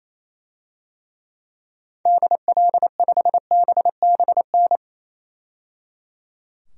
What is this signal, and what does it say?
Morse code, heard as a keyed tone, about 28 wpm, 700 Hz. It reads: DL5BBN
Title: DXCC-100 Call Signs - 6 Characters - 28wpm